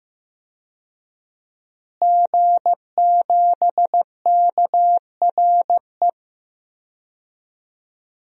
Morse code, heard as a keyed tone, about 15 wpm, 700 Hz. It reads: G7KRE
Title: DXCC-100 Call Signs - 5 Characters - Repeated 3x - 15wpm